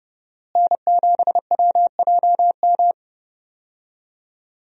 Morse code, heard as a keyed tone, about 30 wpm, 700 Hz. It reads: N7WJM
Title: USA Call Signs - 5 Characters - Repeated 3x - 30wpm